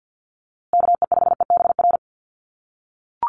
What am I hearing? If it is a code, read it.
Morse code, heard as a keyed tone, about 50 wpm, 700 Hz. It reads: KE5EBR
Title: USA Call Signs Repeated 2x 50wpm